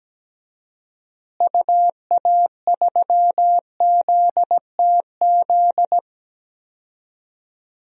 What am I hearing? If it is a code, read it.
Morse code, heard as a keyed tone, about 17 wpm, 700 Hz. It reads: UA3ZTZ
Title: DXCC-100 Call Signs - 6 Characters - Repeated 2x - 17wpm